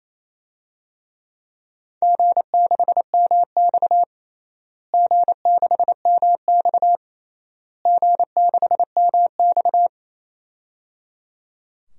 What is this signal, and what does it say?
Morse code, heard as a keyed tone, about 28 wpm, 700 Hz. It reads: G6MX G6MX G6MX
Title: DXCC-100 Call Signs - 4 Characters - Repeated 3x - 28wpm